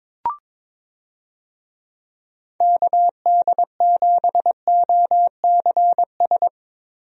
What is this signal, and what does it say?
Morse code, heard as a keyed tone, about 22 wpm, 700 Hz. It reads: KD7OCS
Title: USA Call Signs - 6 Characters - Repeated 3x - 22wpm